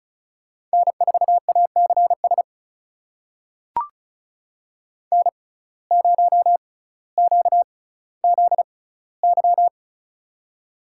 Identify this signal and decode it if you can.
Morse code, heard as a keyed tone, about 36 wpm, spaced as for 18 wpm, 700 Hz. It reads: N4ACS N0QZY
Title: USA Call Signs - 5 Characters - 6x Character Spacing - 35wpm